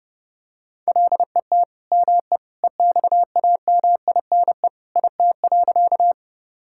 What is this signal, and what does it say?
Morse code, heard as a keyed tone, about 30 wpm, 700 Hz. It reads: LET ME EXAMINE IT.
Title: Sentences from Top 1600 Words 30wpm